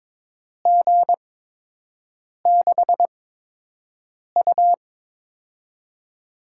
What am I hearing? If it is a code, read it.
Morse code, heard as a keyed tone, about 22 wpm, spaced as for 6 wpm, 700 Hz. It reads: G6U
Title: USA Call Signs - 4 Characters - 8x Character Spacing - 22wpm